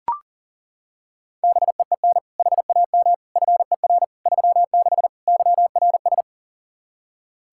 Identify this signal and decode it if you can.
Morse code, heard as a keyed tone, about 40 wpm, 700 Hz. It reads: BEEN HAM FER 36 YRS
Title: Practice QSO - Exchange 3 - More Info - 40wpm